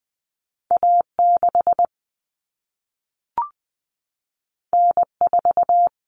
Morse code, heard as a keyed tone, about 20 wpm, 700 Hz. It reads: A6 N4
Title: Common Cut Numbers - 2-Digit 20wpm